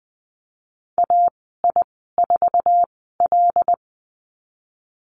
Morse code, heard as a keyed tone, about 20 wpm, spaced as for 14 wpm, 700 Hz. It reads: AI4L
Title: 231 Lesson 37g - Letter x - Call Signs 20wpm